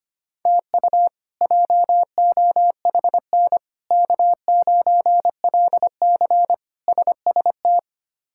Morse code, heard as a keyed tone, about 25 wpm, 700 Hz. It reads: TU JOHN K9LC SST
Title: K1USN SST QSO Practice - 25wpm